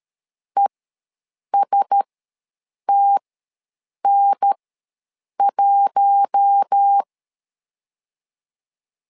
Morse code, heard as a keyed tone, about 13 wpm, spaced as for 7 wpm, 800 Hz. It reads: ESTN1